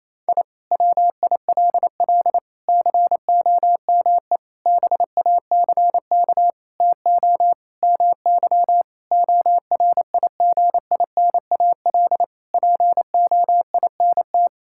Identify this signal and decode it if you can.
Morse code, heard as a keyed tone, about 28 wpm, 700 Hz. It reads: I WILL COME BACK TO MY ORIGINAL POINT
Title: Sentences from Top 1000 Words - Review Part 6 - 28wpm